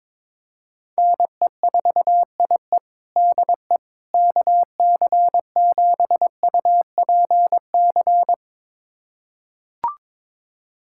Morse code, heard as a keyed tone, about 22 wpm, 700 Hz. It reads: NE4IE DE KC7UPC
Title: USA Callsign DE Callsign 22wpm